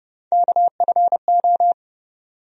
Morse code, heard as a keyed tone, about 30 wpm, 700 Hz. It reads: KFO
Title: USA Call Sign Postfixes - 3 Letters 30wpm